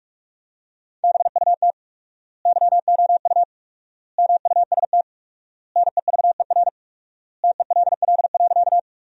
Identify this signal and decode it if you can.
Morse code, heard as a keyed tone, about 46 wpm, 700 Hz. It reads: BUT YOU MUST NEVER TELL.
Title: 4x Word-Spacing - Sentences from Top 300 Words - 45wpm